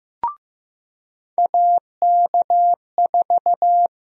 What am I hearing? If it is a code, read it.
Morse code, heard as a keyed tone, about 15 wpm, 700 Hz. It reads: AK4
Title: USA Call Sign Prefixes - 15wpm